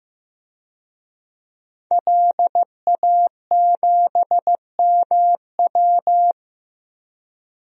Morse code, heard as a keyed tone, about 16 wpm, 700 Hz. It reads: LA7MW
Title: DXCC-100 Call Signs - 5 Characters - 15wpm